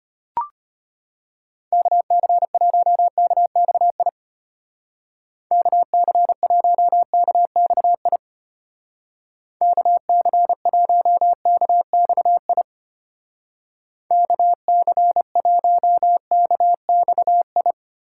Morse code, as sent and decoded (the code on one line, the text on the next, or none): -.- -.-. .---- -.- -..- .. / -.- -.-. .---- -.- -..- .. / -.- -.-. .---- -.- -..- .. / -.- -.-. .---- -.- -..- ..
KC1KXI KC1KXI KC1KXI KC1KXI